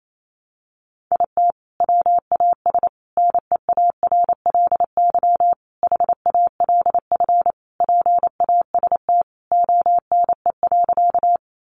Morse code, heard as a keyed tone, about 28 wpm, 700 Hz. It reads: IT WAS NEARLY HALF PAST ONE.
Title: Sentences from Top 1300 Words 28wpm